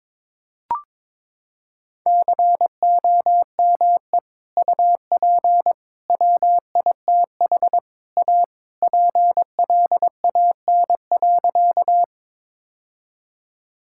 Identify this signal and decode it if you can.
Morse code, heard as a keyed tone, about 22 wpm, 700 Hz. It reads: COME UP WITH A PLAN.
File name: practice / 22wpm / Sentences with 3-Word Phrases - 22wpm